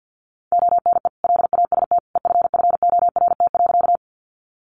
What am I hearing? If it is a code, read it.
Morse code, heard as a keyed tone, about 50 wpm, 700 Hz. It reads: ONE LAST EFFORT.
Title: Sentences from Top 1200 Words 50wpm